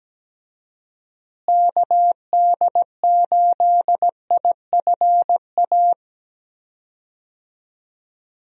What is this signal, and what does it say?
Morse code, heard as a keyed tone, about 17 wpm, 700 Hz. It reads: KD8IFA
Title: USA Call Signs - 6 Characters - Repeated 3x - 17wpm